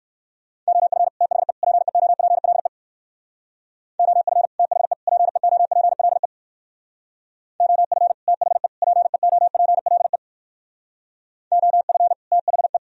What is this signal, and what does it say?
Morse code, heard as a keyed tone, about 56 wpm, 700 Hz. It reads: OF THE PEOPLE OF THE PEOPLE OF THE PEOPLE OF THE
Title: Speed-Racing - Sets of 3 Words 45wpm